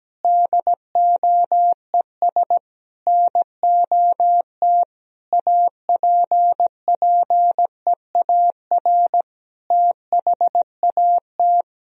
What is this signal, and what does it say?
Morse code, heard as a keyed tone, about 17 wpm, 700 Hz. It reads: DOES NOT APPEAR THAT
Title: Sentences from Top 1000 Words - Review Part 2 - 17wpm